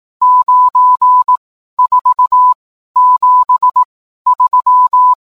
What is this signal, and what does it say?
Morse code, heard as a keyed tone, about 18 wpm, 1000 Hz. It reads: 9 4 7 3
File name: games / morse